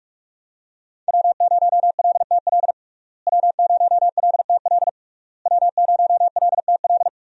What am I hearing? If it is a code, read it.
Morse code, heard as a keyed tone, about 46 wpm, 700 Hz. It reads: W0LTL W0LTL W0LTL
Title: USA Call Signs - 5 Characters - Repeated 3x - 45wpm